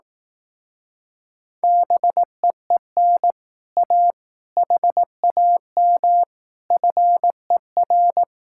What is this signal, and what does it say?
Morse code, heard as a keyed tone, about 18 wpm, 700 Hz. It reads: BEEN A HAM FER
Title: Speed-Racing Common QSO Phrases Encore - 15wpm